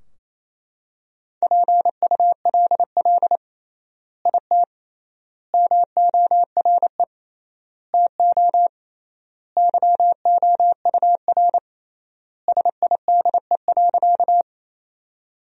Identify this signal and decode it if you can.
Morse code, heard as a keyed tone, about 28 wpm, 700 Hz. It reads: PULL IT MORE TO YOUR SIDE.
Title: 3x Word-Spacing - Sentences from Top 400 Words - 28wpm